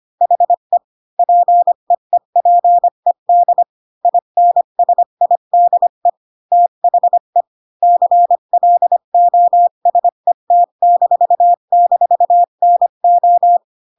Morse code, heard as a keyed tone, about 25 wpm, 700 Hz. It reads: HE PEEPED INSIDE THE CLOSET--NO
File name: Pinocchio / Pinocchio ch1 20 25 ik1hge